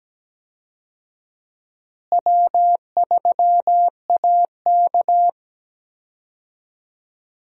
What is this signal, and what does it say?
Morse code, heard as a keyed tone, about 17 wpm, 700 Hz. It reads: W3AK
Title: USA Call Signs - 4 Characters - Repeated 2x - 17wpm